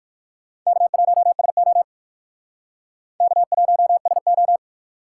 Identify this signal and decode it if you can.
Morse code, heard as a keyed tone, about 65 wpm, spaced as for 57 wpm, 700 Hz. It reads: K1SO K1SO